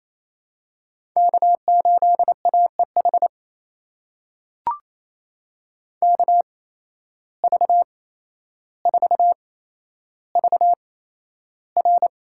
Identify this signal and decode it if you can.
Morse code, heard as a keyed tone, about 28 wpm, spaced as for 12 wpm, 700 Hz. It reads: K8AEH KV4VR
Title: USA Call Signs - 8x Character Spacing - 28wpm